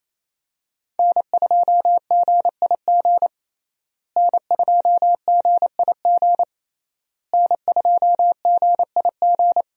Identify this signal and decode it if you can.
Morse code, heard as a keyed tone, about 28 wpm, 700 Hz. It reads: N2GIG N2GIG N2GIG
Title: USA Call Signs - 5 Characters - Repeated 3x - 28wpm